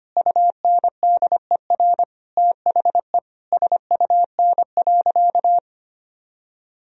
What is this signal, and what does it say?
Morse code, heard as a keyed tone, about 25 wpm, 700 Hz. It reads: UNDER THE SUN.
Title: Sentences from Top 1000 Words - Review Part 1 - 25wpm